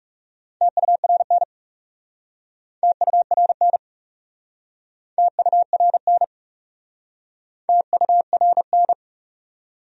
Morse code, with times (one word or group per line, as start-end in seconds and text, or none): 0.6-1.4 TURN
2.8-3.8 TURN
5.2-6.2 TURN
7.7-8.9 TURN